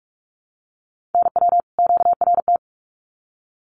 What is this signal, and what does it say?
Morse code, heard as a keyed tone, about 44 wpm, 700 Hz. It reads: NW QRT
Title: Common QSO Phrases 45wpm